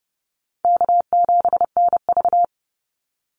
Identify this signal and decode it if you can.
Morse code, heard as a keyed tone, about 30 wpm, 700 Hz. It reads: K7NV
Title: Easy Call Signs - 30wpm